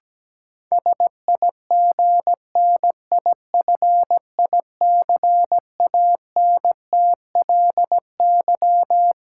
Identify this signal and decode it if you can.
Morse code, heard as a keyed tone, about 17 wpm, 700 Hz. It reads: SIGNIFICANTLY
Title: Sentences from Top 1500 Words - New Word Drill - 17wpm